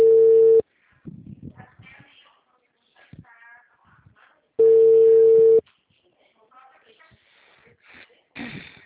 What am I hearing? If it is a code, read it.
Ringback tone (China, 450 Hz), with speech.
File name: out- -2001- - - .52858